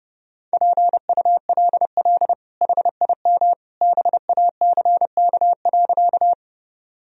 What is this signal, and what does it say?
Morse code, heard as a keyed tone, about 30 wpm, 700 Hz. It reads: PULL HIM BACK.
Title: Sentences from Top 1000 Words - Review Part 2 - 30wpm